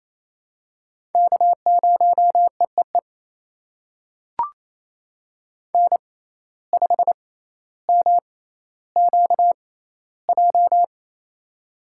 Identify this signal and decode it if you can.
Morse code, heard as a keyed tone, about 27 wpm, spaced as for 14 wpm, 700 Hz. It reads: K0EEE N5MQJ
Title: USA Call Signs - 5 Characters - 6x Character Spacing - 28wpm